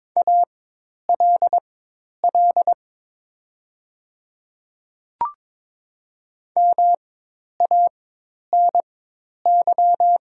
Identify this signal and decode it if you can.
Morse code, heard as a keyed tone, about 22 wpm, spaced as for 10 wpm, 700 Hz. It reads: ALL MANY
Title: 4x Character Spacing - Top 100 Words - 22wpm